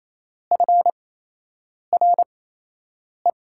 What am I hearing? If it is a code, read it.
Morse code, heard as a keyed tone, about 28 wpm, spaced as for 8 wpm, 700 Hz. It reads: FRE
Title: USA Call Signs - 6 Characters - 8x Character Spacing - 28wpm